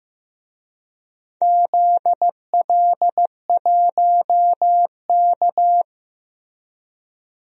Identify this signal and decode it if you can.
Morse code, heard as a keyed tone, about 15 wpm, 700 Hz. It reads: ZL1K
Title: DXCC-100 Call Signs - 4 Characters - Repeated 2x - 15wpm